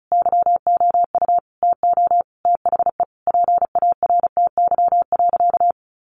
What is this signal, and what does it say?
Morse code, heard as a keyed tone, about 34 wpm, 700 Hz. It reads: YOU TO THE PARTY.
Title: Sentences from Top 1500 Words 35wpm